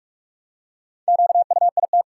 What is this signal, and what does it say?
Morse code, heard as a keyed tone, about 47 wpm, 700 Hz. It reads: QUIT
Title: Four Letter Words - 45wpm